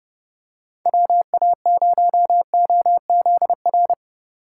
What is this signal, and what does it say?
Morse code, heard as a keyed tone, about 30 wpm, 700 Hz. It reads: WA0OZR
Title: USA Call Signs - 6 Characters - 2x Character Spacing - 30wpm